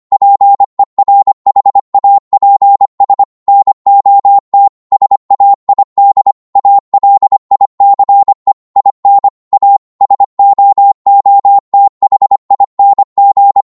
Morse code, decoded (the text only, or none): PERHAPS NOT SAID ALICE IN A SOOTHING